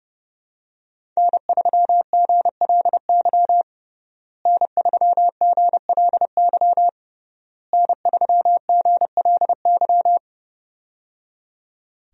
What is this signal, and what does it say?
Morse code, heard as a keyed tone, about 30 wpm, 700 Hz. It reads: N3GLY N3GLY N3GLY
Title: USA Call Signs - 5 Characters - Repeated 3x - 30wpm